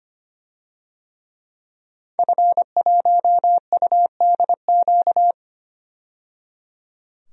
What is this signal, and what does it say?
Morse code, heard as a keyed tone, about 25 wpm, 700 Hz. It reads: F1UDQ
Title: DXCC-100 Call Signs - 5 Characters - 25wpm